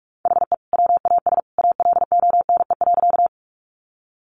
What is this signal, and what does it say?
Morse code, heard as a keyed tone, about 44 wpm, 700 Hz. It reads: HE WAS ALONE.
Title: Sentences from Top 1000 Words 45wpm